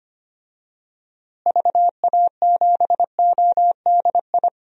Morse code, heard as a keyed tone, about 25 wpm, 700 Hz. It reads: VA7ODI
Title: DXCC-100 Call Signs - 6 Characters - Repeated 2x - 25wpm